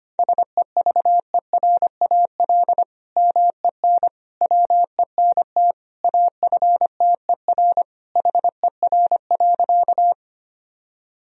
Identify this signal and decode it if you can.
Morse code, heard as a keyed tone, about 25 wpm, 700 Hz. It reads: SEVERAL MEN WENT AFTER HER.
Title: Sentences from Top 1000 Words - Review Part 2 - 25wpm